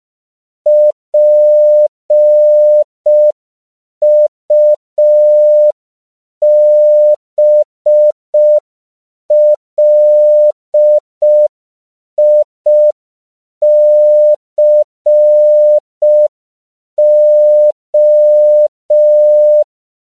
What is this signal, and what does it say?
Morse code, heard as a keyed tone, about 5 wpm, 600 Hz. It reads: PUBLICO
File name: CW / 5PPM